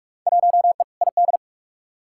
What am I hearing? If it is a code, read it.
Morse code, heard as a keyed tone, about 46 wpm, 700 Hz. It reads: 1E ID
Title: Field Day Exchanges - 45wpm